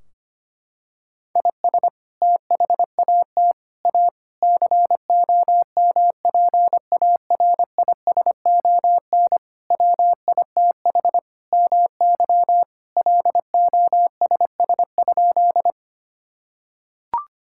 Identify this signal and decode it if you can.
Morse code, heard as a keyed tone, about 25 wpm, 700 Hz. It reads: IS THAT A COMPARISON WITH MY LOSS?
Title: Sentences from Top 1600 Words 25wpm